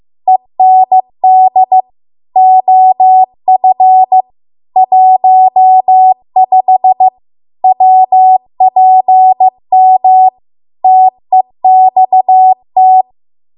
Morse code, heard as a keyed tone, about 15 wpm, 750 Hz. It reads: END OF 15 WPM TEXT